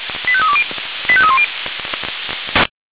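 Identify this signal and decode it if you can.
Five-tone selective call (ZVEI), heard as quick runs of five tones, tones of about 70 ms. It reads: 85320 85310
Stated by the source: ZVEI five-tone selective call received by SDR, from feuerwehr-orlamuende.org (Feuerwehr Orlamünde fire brigade) server.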